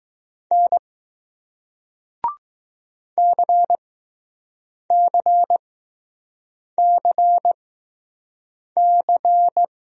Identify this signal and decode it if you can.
Morse code, heard as a keyed tone, about 20 wpm, spaced as for 6 wpm, 700 Hz. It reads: N CCCC